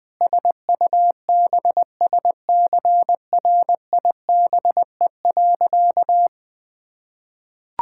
Morse code, heard as a keyed tone, about 21 wpm, 700 Hz. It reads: SUBSCRIBE.
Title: Sentences with 8-Word Phrases - 20wpm